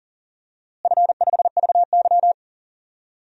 Morse code, heard as a keyed tone, about 40 wpm, 700 Hz. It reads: F5VY
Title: DXCC-100 Call Signs - 4 Characters - Repeated 3x - 40wpm